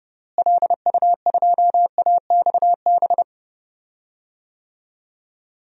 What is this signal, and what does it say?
Morse code, heard as a keyed tone, about 30 wpm, 700 Hz. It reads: LU2AXB
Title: DXCC-100 Call Signs - 6 Characters - Repeated 3x - 30wpm